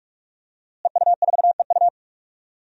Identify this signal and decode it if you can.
Morse code, heard as a keyed tone, about 48 wpm, 700 Hz. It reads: EU4EU
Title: DXCC-100 Call Signs - 5 Characters - Repeated 2x - 45wpm